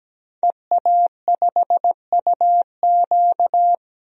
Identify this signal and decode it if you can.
Morse code, heard as a keyed tone, about 17 wpm, 700 Hz. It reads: EA5UQ
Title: DXCC-100 Call Signs - 5 Characters - Repeated 2x - 17wpm